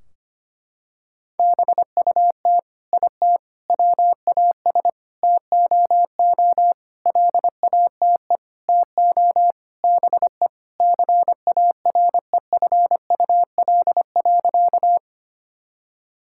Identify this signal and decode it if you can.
Morse code, heard as a keyed tone, about 25 wpm, 700 Hz. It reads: BUT IT WAS TOO LATE TO BE CAREFUL.
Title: Sentences from Top 1600 Words 25wpm